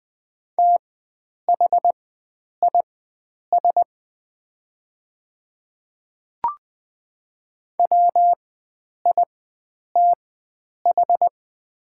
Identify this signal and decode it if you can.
Morse code, heard as a keyed tone, about 20 wpm, spaced as for 9 wpm, 700 Hz. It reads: THIS WITH
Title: 4x Character Spacing - Top 100 Words - 20wpm